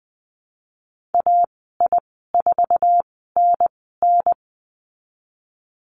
Morse code, heard as a keyed tone, about 20 wpm, spaced as for 14 wpm, 700 Hz. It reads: AI4NN
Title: 055 Lesson 10g - Letter r - Call Signs 20wpm